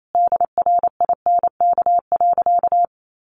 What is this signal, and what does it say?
Morse code, heard as a keyed tone, about 28 wpm, 700 Hz. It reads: DRINK.
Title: Sentences from Top 1500 Words 28wpm